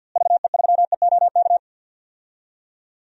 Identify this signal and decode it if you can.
Morse code, heard as a keyed tone, about 46 wpm, 700 Hz. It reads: VE3EOK
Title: DXCC-100 Call Signs - 6 Characters - Repeated 3x - 50wpm